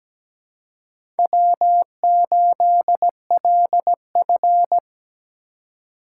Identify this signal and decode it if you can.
Morse code, heard as a keyed tone, about 17 wpm, 700 Hz. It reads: W8LF
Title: USA Call Signs - 4 Characters - 8x Character Spacing - 17wpm